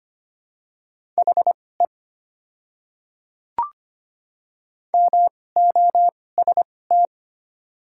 Morse code, heard as a keyed tone, about 25 wpm, spaced as for 18 wpm, 690 Hz. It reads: HE MOST
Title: 2x Character Spacing - Top 100 Words - 25wpm